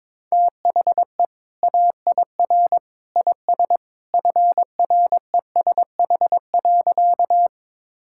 Morse code, heard as a keyed tone, about 21 wpm, 700 Hz. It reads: THE AIR IS FRESH.
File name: practice / 22wpm / Sentences with 2-Word Phrases - 22wpm